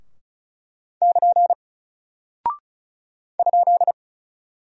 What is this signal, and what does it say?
Morse code, heard as a keyed tone, about 36 wpm, 700 Hz. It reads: <KN> ?
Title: Single Character (Letter:Number:Punctuation:Prosign) Alt 35wpm